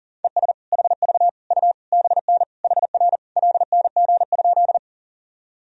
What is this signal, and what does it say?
Morse code, heard as a keyed tone, about 40 wpm, 700 Hz. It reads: ES HV U BN HR LNG?
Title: Sentences from Top 200 Words and Abbreviations 40wpm